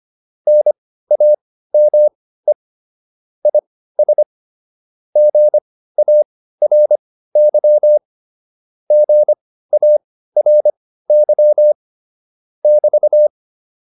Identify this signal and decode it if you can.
Morse code, heard as a keyed tone, about 25 wpm, spaced as for 15 wpm, 600 Hz. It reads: NAME IS GARY GARY <BT>